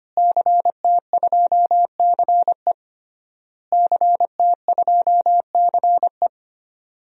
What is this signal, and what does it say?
Morse code, heard as a keyed tone, about 25 wpm, 700 Hz. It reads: CT2CE CT2CE
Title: DXCC-100 Call Signs - 5 Characters - Repeated 3x - 25wpm